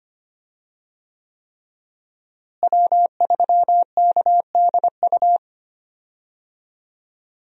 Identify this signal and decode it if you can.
Morse code, heard as a keyed tone, about 25 wpm, 700 Hz. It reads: W3KDU
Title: USA Call Signs - 5 Characters - Repeated 3x - 25wpm